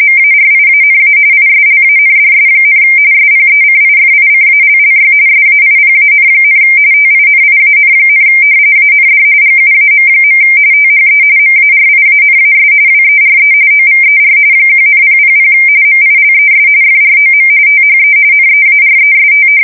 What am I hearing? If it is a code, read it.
RTTY, about 45.45 baud. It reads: MARIANA ISLANDS, SPECIFICALLY SAIPAN, TINIAN, AND GUAM. THE MARIANAS WERE OVER 1,000 MILES FROM THE NEAREST
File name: fsk170 / battleshipnj